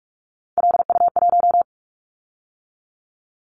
Morse code, heard as a keyed tone, about 46 wpm, 700 Hz. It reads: LU1
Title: Callsign Prefixes for Top 50 DXCC - 45wpm